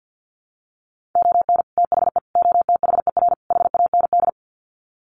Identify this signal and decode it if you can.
Morse code, heard as a keyed tone, about 50 wpm, 700 Hz. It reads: ON THE OTHER HAND
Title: Sets of 4 Words 50wpm